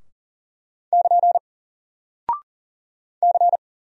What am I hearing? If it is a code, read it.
Morse code, heard as a keyed tone, about 40 wpm, 700 Hz. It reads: <KN> C
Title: Single Character (Letter:Number:Punctuation:Prosign) 40wpm